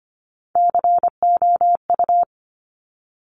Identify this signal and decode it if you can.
Morse code, heard as a keyed tone, about 25 wpm, 700 Hz. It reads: COU
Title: Top Three Letter Combinations in Words 25wpm